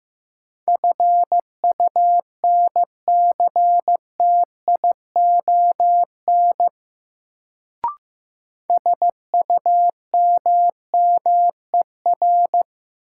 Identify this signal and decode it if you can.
Morse code, heard as a keyed tone, about 15 wpm, 700 Hz. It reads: FUNCTION SUMMER
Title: Sentences from Top 1100 Words - New Word Drill Alt - 15wpm